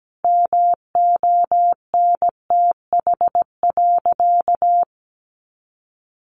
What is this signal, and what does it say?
Morse code, heard as a keyed tone, about 17 wpm, 700 Hz. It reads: MONTH.
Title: Sentences from Top 800 Words 17wpm